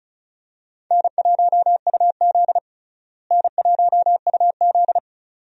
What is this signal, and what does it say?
Morse code, heard as a keyed tone, about 34 wpm, 700 Hz. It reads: N1UZ N1UZ
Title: USA Call Signs - 4 Characters - Repeated 2x - 35wpm